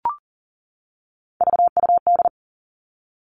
Morse code, heard as a keyed tone, about 42 wpm, 700 Hz. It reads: VUD
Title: Three Challenging Letter-Number Combinations 40wpm